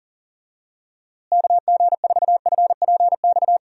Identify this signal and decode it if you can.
Morse code, heard as a keyed tone, about 40 wpm, 700 Hz. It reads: KG4FPX
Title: USA Call Signs - 6 Characters - Repeated 3x - 40wpm